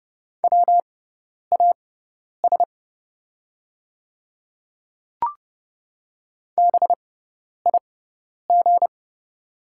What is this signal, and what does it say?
Morse code, heard as a keyed tone, about 30 wpm, spaced as for 10 wpm, 700 Hz. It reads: WAS BIG
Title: 6x Character Spacing - Top 200 Words - 30wpm